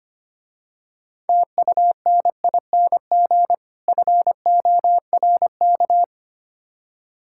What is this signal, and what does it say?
Morse code, heard as a keyed tone, about 28 wpm, 700 Hz. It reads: TUNING FORK
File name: practice / 25wpm / Difficult Words - Speed-Racing with Hints - 25wpm